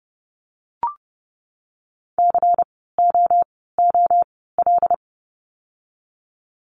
Morse code, heard as a keyed tone, about 30 wpm, spaced as for 17 wpm, 700 Hz. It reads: COOL
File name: Lesson 16f - Letter c - Words 30wpm